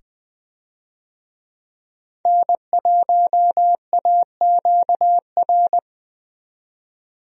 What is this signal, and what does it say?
Morse code, heard as a keyed tone, about 20 wpm, 700 Hz. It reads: N1AQR
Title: USA Call Signs - 5 Characters - 20wpm